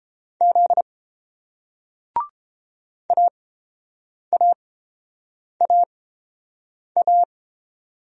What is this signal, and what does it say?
Morse code, heard as a keyed tone, about 30 wpm, spaced as for 7 wpm, 690 Hz. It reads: Z AAAA